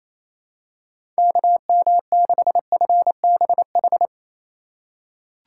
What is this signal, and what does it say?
Morse code, heard as a keyed tone, about 28 wpm, 700 Hz. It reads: KM6FBH
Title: USA Call Signs - 6 Characters - 3x Character Spacing - 28wpm